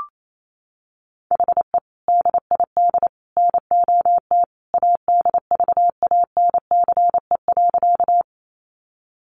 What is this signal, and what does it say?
Morse code, heard as a keyed tone, about 28 wpm, 700 Hz. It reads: HE DID NOT ADVANCE.
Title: Sentences from Top 1400 Words 28wpm